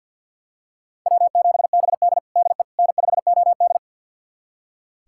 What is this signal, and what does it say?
Morse code, heard as a keyed tone, about 46 wpm, 700 Hz. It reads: W7BD DE N5OD
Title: USA Callsign DE Callsign - 4 Characters - 50wpm